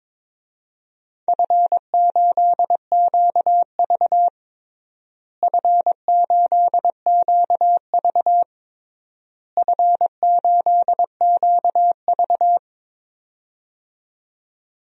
Morse code, heard as a keyed tone, about 22 wpm, 700 Hz. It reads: F8QV F8QV F8QV
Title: DXCC-100 Call Signs - 4 Characters - Repeated 3x - 22wpm